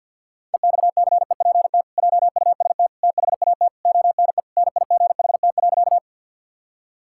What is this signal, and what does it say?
Morse code, heard as a keyed tone, about 52 wpm, 700 Hz. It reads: EXCEPT JUST THAT ONE NIGHT.
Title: Sentences from Top 1600 Words 50wpm